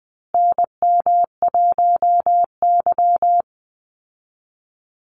Morse code, heard as a keyed tone, about 20 wpm, 700 Hz. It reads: NM1Y
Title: Call Signs Repeated 2x - 20wpm